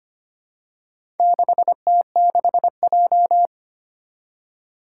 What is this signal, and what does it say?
Morse code, heard as a keyed tone, about 25 wpm, 700 Hz. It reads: 6T6J
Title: Four Challenging Letter-Number Combinations 25wpm